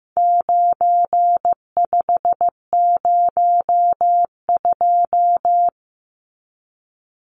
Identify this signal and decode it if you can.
Morse code, heard as a keyed tone, about 15 wpm, 700 Hz. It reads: 9502
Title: Numbers - 4-Digit 15wpm